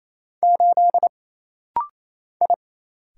Morse code, heard as a keyed tone, about 27 wpm, 700 Hz. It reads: 8 I
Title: Single Letter-Number - Rapid-Fire 28wpm